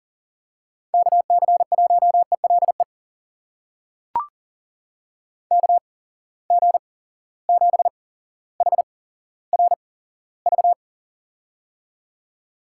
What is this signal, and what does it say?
Morse code, heard as a keyed tone, about 40 wpm, spaced as for 17 wpm, 700 Hz. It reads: KC1ELE KG7HRV
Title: USA Call Signs - 6 Characters - 8x Character Spacing - 40wpm